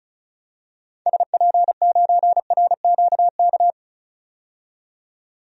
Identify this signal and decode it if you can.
Morse code, heard as a keyed tone, about 36 wpm, 700 Hz. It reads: SP9RQK
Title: DXCC-100 Call Signs - 6 Characters - 35wpm